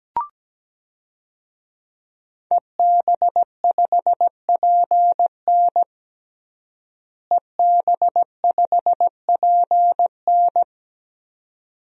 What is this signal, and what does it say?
Morse code, heard as a keyed tone, about 17 wpm, 700 Hz. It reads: EB5PN EB5PN
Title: DXCC-100 Call Signs - 5 Characters - Repeated 2x - 17wpm